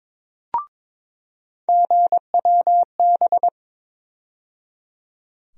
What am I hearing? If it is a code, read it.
Morse code, heard as a keyed tone, about 22 wpm, 700 Hz. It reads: GWB
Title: USA Call Sign Postfixes - 3 Letters 22wpm